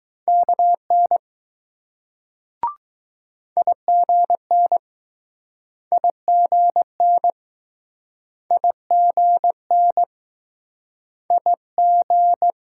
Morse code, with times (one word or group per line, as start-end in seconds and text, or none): 0.3-1.2 KN
3.6-4.8 IGN
5.9-7.3 IGN
8.5-10.0 IGN
11.3-11.5 I
11.8-12.5 G